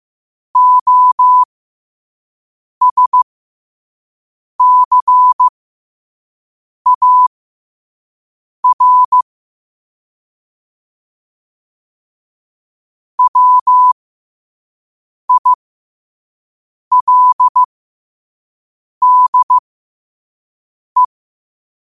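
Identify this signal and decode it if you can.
Morse code, heard as a keyed tone, about 15 wpm, spaced as for 5 wpm, 1000 Hz. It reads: OSCAR WILDE
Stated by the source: Morse code quote of the day at 5 wpm.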